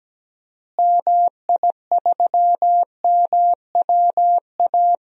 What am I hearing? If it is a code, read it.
Morse code, heard as a keyed tone, about 17 wpm, 700 Hz. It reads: MI3MWA
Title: DXCC-100 Call Signs - 6 Characters - Repeated 3x - 17wpm